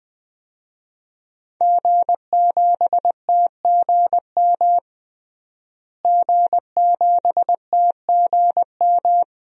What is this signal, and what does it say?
Morse code, heard as a keyed tone, about 20 wpm, 700 Hz. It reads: G7TGM G7TGM
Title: DXCC-100 Call Signs - 5 Characters - Repeated 2x - 20wpm